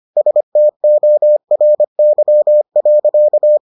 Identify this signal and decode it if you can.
Morse code, heard as a keyed tone, about 25 wpm, 600 Hz. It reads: STORY.